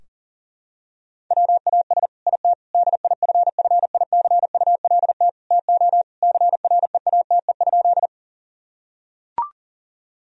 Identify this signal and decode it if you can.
Morse code, heard as a keyed tone, about 40 wpm, 700 Hz. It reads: WAS IT DIFFICULT TO CREATE?
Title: Sentences from Top 1000 Words - Review Part 3 - 40wpm